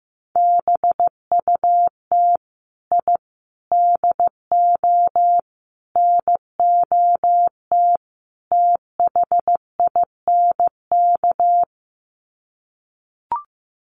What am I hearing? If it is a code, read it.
Morse code, heard as a keyed tone, about 15 wpm, 700 Hz. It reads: BUT I DO NOT THINK
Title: Sets of 5 Words 15wpm